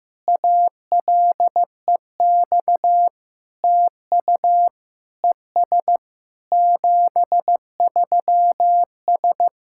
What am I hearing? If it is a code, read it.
Morse code, heard as a keyed tone, about 15 wpm, 700 Hz. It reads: ALEX TU ES 73S
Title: SOTA QSO Practice 15wpm